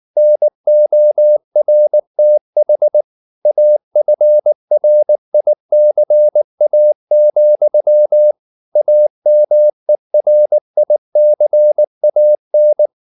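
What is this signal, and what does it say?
Morse code, heard as a keyed tone, about 19 wpm, 600 Hz. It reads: NORTH AFRICA, AMERICAN
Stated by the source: Morse code at 19 wpm from CW med Gnister, reading Ken Follett's novel Never in English.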